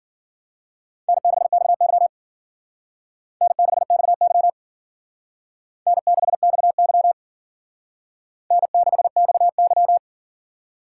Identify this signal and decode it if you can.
Morse code, heard as a keyed tone, about 51 wpm, 700 Hz. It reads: N6XY N6XY N6XY N6XY